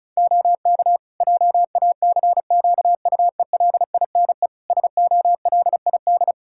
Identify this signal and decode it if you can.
Morse code, heard as a keyed tone, about 36 wpm, 700 Hz. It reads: OK JACQUELINE SOLID
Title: K1USN SST QSO Practice 35wpm